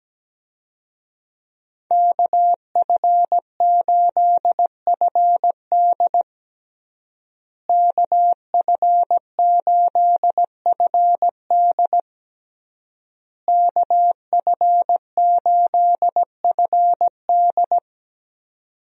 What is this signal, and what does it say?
Morse code, heard as a keyed tone, about 17 wpm, 700 Hz. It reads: KF8FD KF8FD KF8FD
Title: USA Call Signs - 5 Characters - Repeated 3x - 17wpm